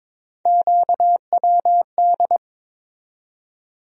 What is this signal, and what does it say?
Morse code, heard as a keyed tone, about 22 wpm, 700 Hz. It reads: QWD
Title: USA Call Sign Postfixes - 3 Letters 22wpm